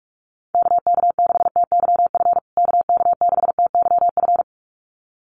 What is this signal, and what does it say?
Morse code, heard as a keyed tone, about 44 wpm, 700 Hz. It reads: KK6TYF KK6TYF
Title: USA Call Signs Repeated 2x 45wpm